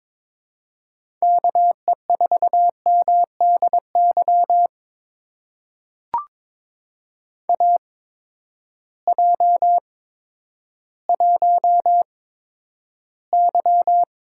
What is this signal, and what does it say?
Morse code, heard as a keyed tone, about 22 wpm, spaced as for 11 wpm, 700 Hz. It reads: KE4MDY AJ1Y
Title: USA Call Signs - 8x Character Spacing - 22wpm